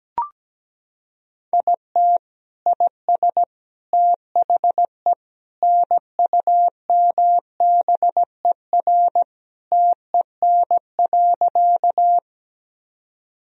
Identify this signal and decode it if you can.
Morse code, heard as a keyed tone, about 17 wpm, 700 Hz. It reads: IT IS THE NUMBER TEN.
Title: Sentences from Top 1000 Words - Review Part 2 - 17wpm